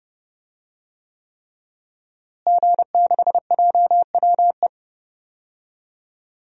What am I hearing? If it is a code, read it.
Morse code, heard as a keyed tone, about 30 wpm, 700 Hz. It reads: G6JWE
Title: DXCC-100 Call Signs - 5 Characters - Repeated 3x - 30wpm